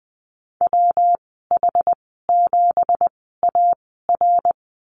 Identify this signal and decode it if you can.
Morse code, heard as a keyed tone, about 20 wpm, spaced as for 14 wpm, 700 Hz. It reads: WH7AR
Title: 193 Lesson 31g - Letter v - Call Signs 20wpm